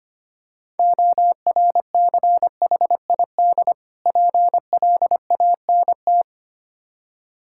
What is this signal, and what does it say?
Morse code, heard as a keyed tone, about 25 wpm, 700 Hz. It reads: ORCHID PLANT
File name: practice / 25wpm / Difficult Words with Hints - 25wpm